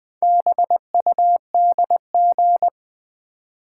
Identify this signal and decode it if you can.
Morse code, heard as a keyed tone, about 20 wpm, 690 Hz. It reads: BUDG
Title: Build A Word - 2 Letters At A Time - 6-Letter Words - 20wpm